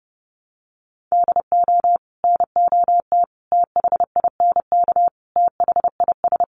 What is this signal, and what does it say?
Morse code, heard as a keyed tone, about 30 wpm, 700 Hz. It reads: DO NOT THINK THIS
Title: Sets of 5 Words Encore2 30wpm